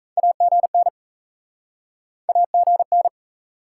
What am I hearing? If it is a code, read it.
Morse code, heard as a keyed tone, about 45 wpm, 700 Hz. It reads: AGN AGN